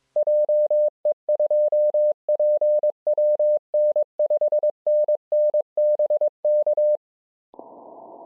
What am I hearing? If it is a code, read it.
Morse code, heard as a keyed tone, about 22 wpm, 600 Hz. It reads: JE2PWN 5NN BK
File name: je2pwn-g8bcg